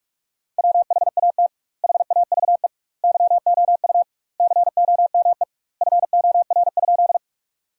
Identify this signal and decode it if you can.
Morse code, heard as a keyed tone, about 43 wpm, 700 Hz. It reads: WHAT HAVE YOU COME FOR?
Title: 2x Word-Spacing - Sentences from Top 100 Words - 45wpm